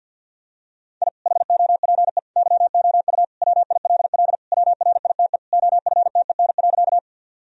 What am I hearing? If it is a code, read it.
Morse code, heard as a keyed tone, about 47 wpm, 700 Hz. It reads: I HOPE YOU WILL WRITE OFTEN.
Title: Sentences from Top 1000 Words - Review Part 1 - 50wpm